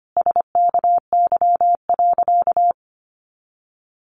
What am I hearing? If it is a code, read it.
Morse code, heard as a keyed tone, about 25 wpm, 700 Hz. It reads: SKY.
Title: Additional Easy Phrases 25wpm